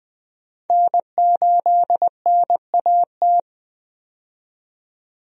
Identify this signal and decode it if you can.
Morse code, heard as a keyed tone, about 20 wpm, 700 Hz. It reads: N8NAT
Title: USA Call Signs - 5 Characters - 20wpm